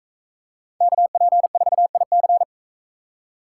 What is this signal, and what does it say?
Morse code, heard as a keyed tone, about 42 wpm, 700 Hz. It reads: KP4IC